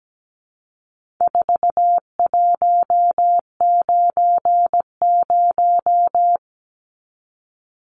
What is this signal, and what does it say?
Morse code, heard as a keyed tone, about 17 wpm, 700 Hz. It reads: 4190
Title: Numbers - 4-Digit 17wpm